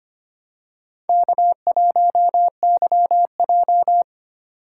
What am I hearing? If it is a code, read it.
Morse code, heard as a keyed tone, about 25 wpm, 700 Hz. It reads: K1YJ
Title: USA Call Signs - 4 Characters - 3x Character Spacing - 25wpm